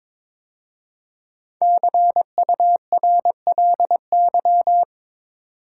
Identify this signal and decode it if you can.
Morse code, heard as a keyed tone, about 22 wpm, 700 Hz. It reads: CURLY
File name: 5 Letter Words - First 3 Letters Given - 22wpm